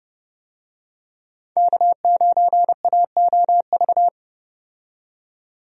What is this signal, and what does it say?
Morse code, heard as a keyed tone, about 32 wpm, 700 Hz. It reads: K9AOV
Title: USA Call Signs - 5 Characters - 30wpm